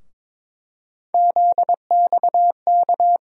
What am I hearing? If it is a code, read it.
Morse code, heard as a keyed tone, about 22 wpm, 700 Hz. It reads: ZXK
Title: USA Call Sign Postfixes - 3 Letters 22wpm